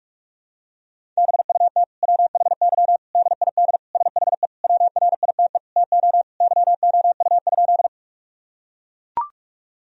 Morse code, heard as a keyed tone, about 41 wpm, 700 Hz. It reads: BUT WHY DID SHE WRITE TO YOU?
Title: Sentences from Top 1000 Words - Review Part 1 - 45wpm